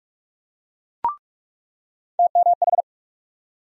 Morse code, heard as a keyed tone, about 42 wpm, 700 Hz. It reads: TMH
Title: USA Call Sign Postfixes - 3 Letters 45wpm